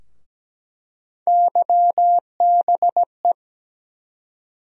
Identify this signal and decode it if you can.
Morse code, heard as a keyed tone, about 17 wpm, 700 Hz. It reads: YBE
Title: USA Call Sign Postfixes - 3 Letters 17wpm